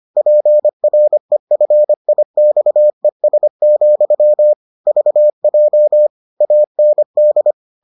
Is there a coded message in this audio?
Yes: Morse code from 0.2 to 7.5 s, heard as a keyed tone, about 25 wpm, 600 Hz. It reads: PREFIXES, VJ AND